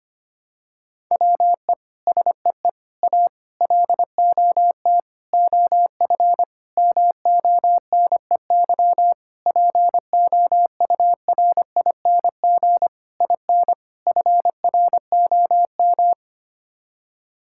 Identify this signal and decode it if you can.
Morse code, heard as a keyed tone, about 25 wpm, 700 Hz. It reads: WE SEE A LOT OF MONEY POURING IN FROM
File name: Sets of 9 Words - 25wpm